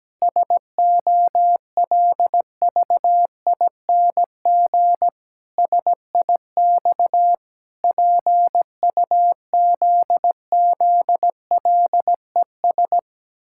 Morse code, heard as a keyed tone, about 17 wpm, 700 Hz. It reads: SOLVING SIX PUZZLES
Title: Pangram Challenge v2 - 17wpm